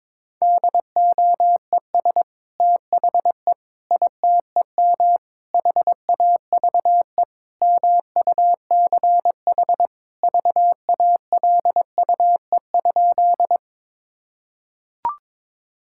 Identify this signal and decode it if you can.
Morse code, heard as a keyed tone, about 22 wpm, 700 Hz. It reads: DOES THE ITEM HAVE MUCH VALUE?
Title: Sentences from Top 1000 Words - Review Part 6 - 22wpm